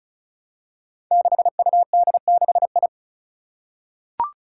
Morse code, heard as a keyed tone, about 36 wpm, 700 Hz. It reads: 6UD6I
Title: Five Challenging Letter-Number Combinations 35wpm